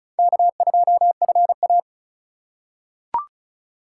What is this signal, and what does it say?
Morse code, heard as a keyed tone, about 36 wpm, 700 Hz. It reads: K2FA
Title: USA Call Signs - 4 Characters - 4x Character Spacing - 35wpm